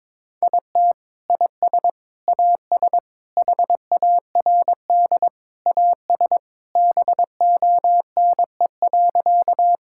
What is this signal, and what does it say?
Morse code, heard as a keyed tone, about 22 wpm, 700 Hz. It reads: IT IS AS HARD AS BONE.
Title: Sentences from Top 1600 Words 22wpm